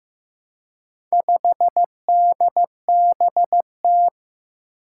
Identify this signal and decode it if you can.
Morse code, heard as a keyed tone, about 15 wpm, 700 Hz. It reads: 5DBT
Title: Four Challenging Letter-Number Combinations 15wpm